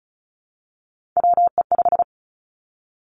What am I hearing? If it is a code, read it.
Morse code, heard as a keyed tone, about 36 wpm, 700 Hz. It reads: WE5
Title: Three Challenging Letter-Number Combinations 35wpm